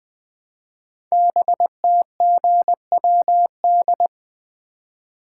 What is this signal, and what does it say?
Morse code, heard as a keyed tone, about 20 wpm, 700 Hz. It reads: BTGWD
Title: Code Groups - 5 Letters - 20wpm